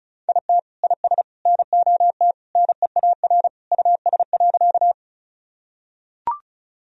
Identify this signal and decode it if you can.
Morse code, heard as a keyed tone, about 36 wpm, 700 Hz. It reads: IT IS NOT NEAR US.
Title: Sentences from Top 1000 Words - Review Part 1 - 35wpm